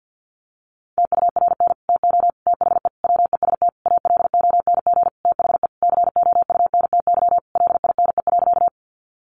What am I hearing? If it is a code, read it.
Morse code, heard as a keyed tone, about 50 wpm, 700 Hz. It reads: TURN TO THE WEST ALONG THE COUNTY LINE.
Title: Sentences from Top 1200 Words 50wpm